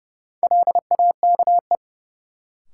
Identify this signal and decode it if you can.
Morse code, heard as a keyed tone, about 30 wpm, 700 Hz. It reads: LAKE
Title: Sentences from Top 1100 Words - New Word Drill Alt - 30wpm